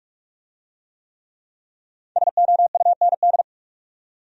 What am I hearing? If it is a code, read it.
Morse code, heard as a keyed tone, about 42 wpm, 700 Hz. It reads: SOUND